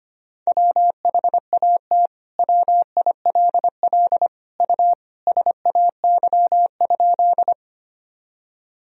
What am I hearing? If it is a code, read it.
Morse code, heard as a keyed tone, about 25 wpm, 700 Hz. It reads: WHAT WILL U SAY?
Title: Sentences from Top 200 Words and Abbreviations 25wpm